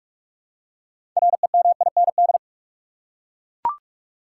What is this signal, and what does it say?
Morse code, heard as a keyed tone, about 41 wpm, 700 Hz. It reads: REMIND
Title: Six Letter Words - 45wpm